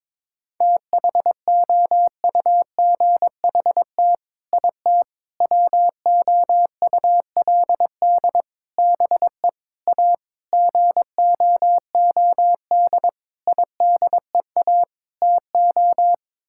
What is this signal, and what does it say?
Morse code, heard as a keyed tone, about 21 wpm, 700 Hz. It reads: THOUGHT IT WOULD BE A GOOD IDEA TO
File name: practice / 22wpm / Sets of 8 Words - 22wpm